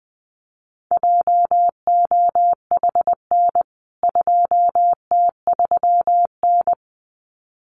Morse code, heard as a keyed tone, about 19 wpm, 700 Hz. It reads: JOHN 2T3N
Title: CWT Exchange 20wpm